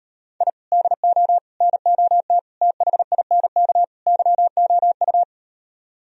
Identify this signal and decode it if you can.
Morse code, heard as a keyed tone, about 38 wpm, 700 Hz. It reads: I DO NOT THINK YOU